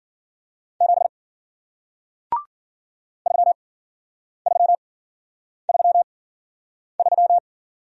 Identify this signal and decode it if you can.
Morse code, heard as a keyed tone, about 53 wpm, spaced as for 9 wpm, 700 Hz. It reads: 7 3333